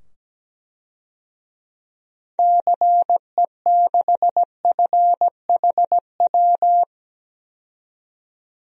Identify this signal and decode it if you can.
Morse code, heard as a keyed tone, about 17 wpm, 700 Hz. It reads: CE6FHW
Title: DXCC-100 Call Signs - 6 Characters - 17wpm